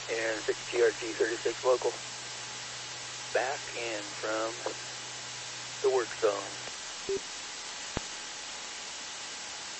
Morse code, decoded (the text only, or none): none